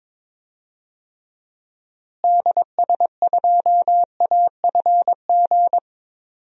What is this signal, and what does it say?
Morse code, heard as a keyed tone, about 22 wpm, 700 Hz. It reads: DS2AFG
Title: DXCC-100 Call Signs - 6 Characters - Repeated 2x - 22wpm